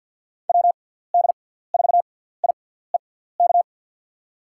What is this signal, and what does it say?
Morse code, heard as a keyed tone, about 53 wpm, spaced as for 17 wpm, 690 Hz. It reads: WD4IEK
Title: USA Call Signs - 6x Character Spacing - 50wpm